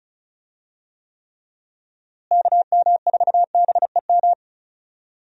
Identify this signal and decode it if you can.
Morse code, heard as a keyed tone, about 36 wpm, 700 Hz. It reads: KM4BEM
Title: USA Call Signs - 6 Characters - Repeated 2x - 35wpm